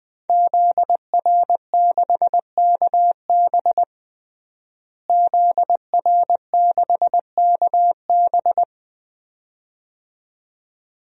Morse code, heard as a keyed tone, about 20 wpm, 700 Hz. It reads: ZR6KB ZR6KB
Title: DXCC-100 Call Signs - 5 Characters - Repeated 2x - 20wpm